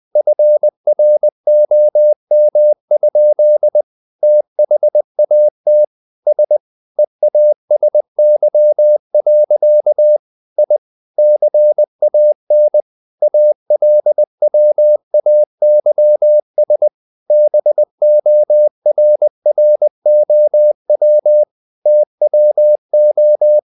Morse code, heard as a keyed tone, about 20 wpm, 600 Hz. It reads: FROM? THAT S EASY. I CAN ALWAYS BORROW TWO
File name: the old man and the sea / the old man and the sea part 1 20wpm